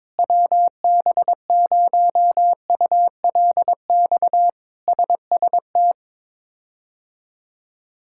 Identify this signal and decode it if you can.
Morse code, heard as a keyed tone, about 22 wpm, 700 Hz. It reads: WB0ULX SST
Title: K1USN SST QSO Practice - 22wpm